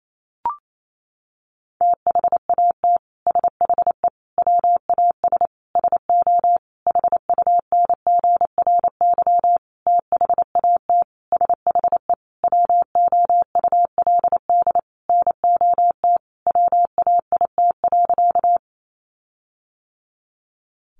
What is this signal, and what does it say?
Morse code, heard as a keyed tone, about 28 wpm, 700 Hz. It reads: THAT SHE WAS SO HUNGRY THAT SHE WOULD NOT WAIT.
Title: Sentences from Top 1400 Words 28wpm